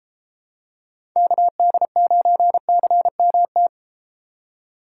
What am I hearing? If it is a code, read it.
Morse code, heard as a keyed tone, about 33 wpm, 700 Hz. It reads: KD9CMT